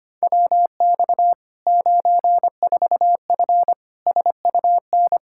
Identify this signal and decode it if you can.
Morse code, heard as a keyed tone, about 25 wpm, 700 Hz. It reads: WX 94F SUN
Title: Practice QSO - Exchange 2 - Info - 25wpm